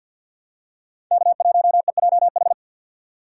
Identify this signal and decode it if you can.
Morse code, heard as a keyed tone, about 51 wpm, 700 Hz. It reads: K1EJH
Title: USA Call Signs - 5 Characters - 4x Character Spacing - 50wpm